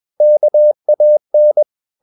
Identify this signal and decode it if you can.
Morse code, heard as a keyed tone, about 21 wpm, 600 Hz. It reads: KAN
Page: Ferie 20 wpm